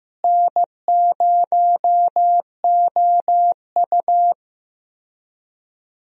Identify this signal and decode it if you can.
Morse code, heard as a keyed tone, about 15 wpm, 700 Hz. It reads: N0OU
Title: USA Call Signs - 4 Characters - Repeated 3x - 15wpm